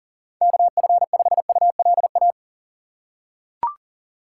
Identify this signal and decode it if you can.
Morse code, heard as a keyed tone, about 40 wpm, 700 Hz. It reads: KF5ULA
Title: USA Call Signs - 4x Character Spacing - 40wpm